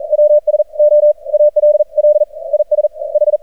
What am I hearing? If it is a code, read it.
Morse code, heard as a keyed tone, about 44 wpm, spaced as for 37 wpm, 600 Hz. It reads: W SO ALL IS H
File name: archived / CRSnet